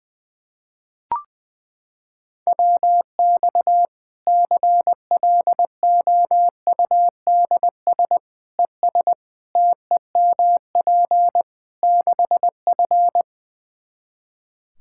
Morse code, heard as a keyed tone, about 20 wpm, 700 Hz. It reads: WX CLOUDS ES TEMP 6F
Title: Practice QSO - Exchange 2 - Info - 20wpm